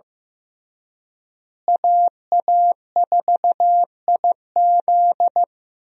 Morse code, heard as a keyed tone, about 16 wpm, 700 Hz. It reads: AA4IZ
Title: USA Call Signs - 5 Characters - Repeated 3x - 15wpm